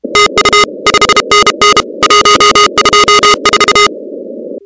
Morse code, heard as a keyed tone, about 32 wpm, 1240 Hz, with noise. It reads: TU 5NN 124
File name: contests / 2019-cw-wpx-kt5j-k5pi- -2330 0-16